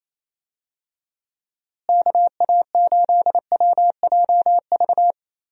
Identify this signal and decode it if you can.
Morse code, heard as a keyed tone, about 28 wpm, 700 Hz. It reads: KA8WJV
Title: USA Call Signs - 6 Characters - Repeated 2x - 28wpm